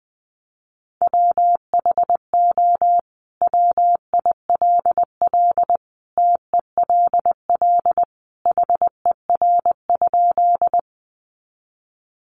Morse code, heard as a keyed tone, about 20 wpm, 700 Hz. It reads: WHO WILL TELL HER?
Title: Sentences from Top 200 Words 20wpm